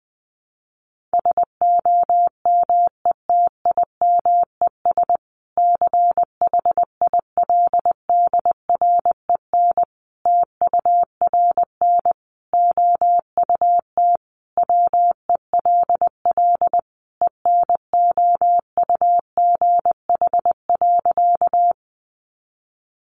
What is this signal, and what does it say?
Morse code, heard as a keyed tone, about 19 wpm, 700 Hz. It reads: SOMETIMES CHILDREN TURN OUT WELL ENOUGH.
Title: Sentences from Top 600 Words 20wpm